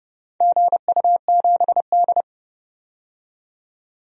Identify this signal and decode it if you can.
Morse code, heard as a keyed tone, about 29 wpm, 700 Hz. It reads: GU7D
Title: DXCC-100 Call Signs - 4 Characters - 30wpm